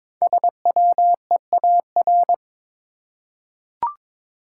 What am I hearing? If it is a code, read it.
Morse code, heard as a keyed tone, about 22 wpm, 700 Hz. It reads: SWEAR
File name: 5 Letter Words - First 3 Letters Given - 22wpm